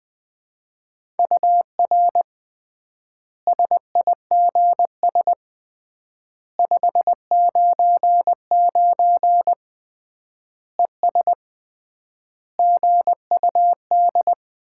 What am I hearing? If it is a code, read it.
Morse code, heard as a keyed tone, about 20 wpm, 700 Hz. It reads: UR SIGS 599 ES GUD
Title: Common QSO Phrases - 3x Word-Spacing - 20wpm